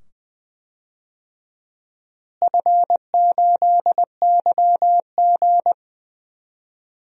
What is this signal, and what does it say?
Morse code, heard as a keyed tone, about 21 wpm, 700 Hz. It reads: F8YG
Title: DXCC-100 Call Signs - 4 Characters - Repeated 2x - 20wpm